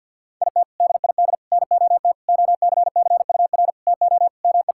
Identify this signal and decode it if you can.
Morse code, heard as a keyed tone, about 50 wpm, 700 Hz. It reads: IT DID NOT OCCUR TO ME
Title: Sentences from Top 1000 Words - Review Part 6 - 50wpm